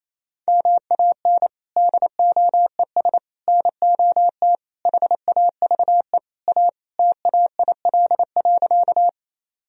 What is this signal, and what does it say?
Morse code, heard as a keyed tone, about 28 wpm, 700 Hz. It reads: MAN DOES NOT HAVE A TAIL.
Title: Sentences from Top 1000 Words - Review Part 3 - 28wpm